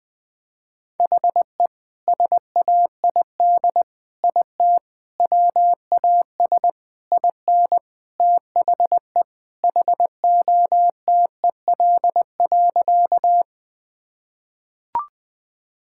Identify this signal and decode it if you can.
Morse code, heard as a keyed tone, about 20 wpm, 700 Hz. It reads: HE SAID IT WAS IN THE HOTEL.
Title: Sentences from Top 1600 Words 20wpm